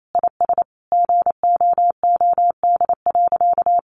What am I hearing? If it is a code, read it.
Morse code, heard as a keyed tone, about 28 wpm, 700 Hz. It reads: IS GOOD.
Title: Sentences from Top 800 Words 28wpm